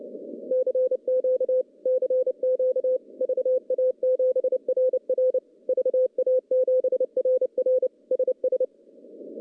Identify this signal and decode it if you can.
Morse code, heard as a keyed tone, about 30 wpm, 510 Hz. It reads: CQ CQ VA7RR VA7RR SS
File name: contests / CW-SS-k5tr- -0900